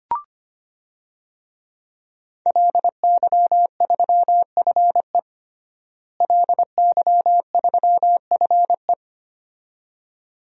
Morse code, heard as a keyed tone, about 25 wpm, 700 Hz. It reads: LY3FE LY3FE
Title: DXCC-100 Call Signs - 5 Characters - Repeated 2x - 25wpm